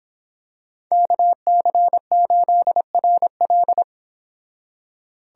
Morse code, heard as a keyed tone, about 26 wpm, 700 Hz. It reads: KC8RL
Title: Speed Racing - USA 2x2 Call Signs 22wpm